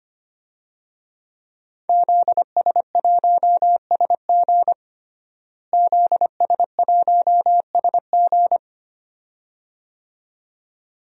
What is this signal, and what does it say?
Morse code, heard as a keyed tone, about 25 wpm, 700 Hz. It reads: ZS1SG ZS1SG
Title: DXCC-100 Call Signs - 5 Characters - Repeated 2x - 25wpm